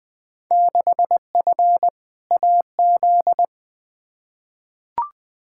Morse code, heard as a keyed tone, about 20 wpm, 700 Hz. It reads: 6F AZ
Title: Field Day Exchanges Encore - 20wpm